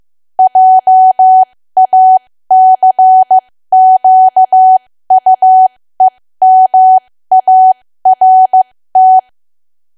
Morse code, heard as a keyed tone, about 15 wpm, spaced as for 13 wpm, 750 Hz. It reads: JACQUEMART
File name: cwa / W1AW-Level2 / 13WPM